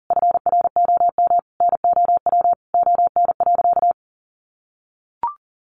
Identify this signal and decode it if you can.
Morse code, heard as a keyed tone, about 37 wpm, 700 Hz. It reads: FROM NOW ON.
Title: Easy Phrases 40wpm